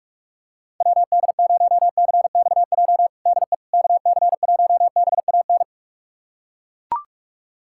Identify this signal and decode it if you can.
Morse code, heard as a keyed tone, about 41 wpm, 690 Hz. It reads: WD0CXJ DE KC1BAN
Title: USA Callsign DE Callsign - 6 Characters - 45wpm